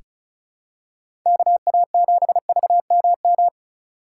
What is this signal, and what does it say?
Morse code, heard as a keyed tone, about 34 wpm, 700 Hz. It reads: KA7VMM
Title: USA Call Signs - 6 Characters - 4x Character Spacing - 35wpm